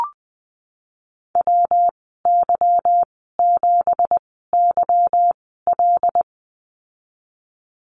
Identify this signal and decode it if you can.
Morse code, heard as a keyed tone, about 20 wpm, spaced as for 15 wpm, 700 Hz. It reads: WY7YL
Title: 173 Lesson 28g - Number 9 - Call Signs 20wpm